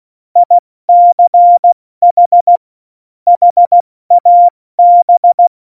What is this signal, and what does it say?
Morse code, heard as a keyed tone, about 16 wpm, 700 Hz. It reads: ICH HAB
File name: Morsekurs / Witze / Witz 08. Dezember 2020, 70 F80